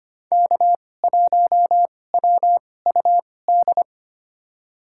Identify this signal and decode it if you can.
Morse code, heard as a keyed tone, about 25 wpm, spaced as for 18 wpm, 700 Hz. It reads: K1WUD
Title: USA Call Signs - 5 Characters - 2x Character Spacing - 25wpm